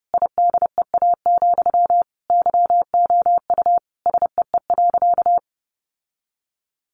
Morse code, heard as a keyed tone, about 30 wpm, 700 Hz. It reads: IDEA, YOU SEE.
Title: Sentences from Top 600 Words 30wpm